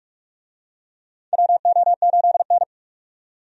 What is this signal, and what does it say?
Morse code, heard as a keyed tone, about 47 wpm, 700 Hz. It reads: WO8N
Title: USA Call Signs - 4 Characters - 45wpm